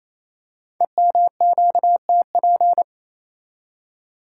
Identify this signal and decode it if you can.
Morse code, heard as a keyed tone, about 27 wpm, 700 Hz. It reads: EMQTP
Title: Code Groups - 5 Letters - 28wpm